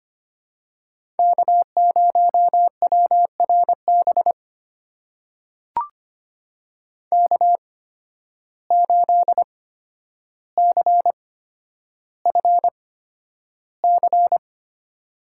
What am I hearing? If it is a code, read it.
Morse code, heard as a keyed tone, about 25 wpm, spaced as for 11 wpm, 700 Hz. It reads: K0WRB K8CFC
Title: USA Call Signs - 5 Characters - 8x Character Spacing - 25wpm